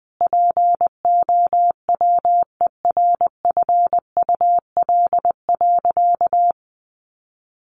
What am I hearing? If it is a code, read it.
Morse code, heard as a keyed tone, about 20 wpm, 700 Hz. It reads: POWERFUL.
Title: Sentences from Top 1300 Words 20wpm